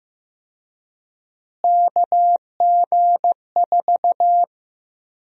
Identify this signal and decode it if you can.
Morse code, heard as a keyed tone, about 16 wpm, 700 Hz. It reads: KG4
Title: USA Call Sign Prefixes - Repeated 2x - 15wpm